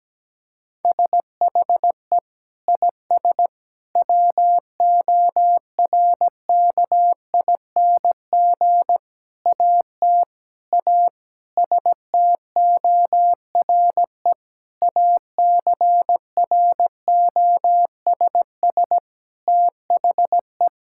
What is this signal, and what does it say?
Morse code, heard as a keyed tone, about 17 wpm, 700 Hz. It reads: SHE IS WORKING AT A STORE ACROSS THE
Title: Sentences from Top 1000 Words - Review Part 5 - 17wpm